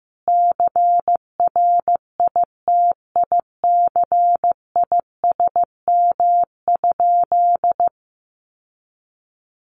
Morse code, heard as a keyed tone, about 15 wpm, 700 Hz. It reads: CRITICISM?
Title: Sentences from Top 1300 Words 15wpm